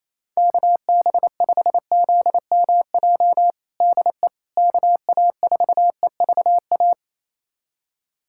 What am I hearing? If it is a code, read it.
Morse code, heard as a keyed tone, about 28 wpm, 700 Hz. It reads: KB5ZMJ DE KA4EVA
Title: USA Callsign DE Callsign - 6 Characters - 28wpm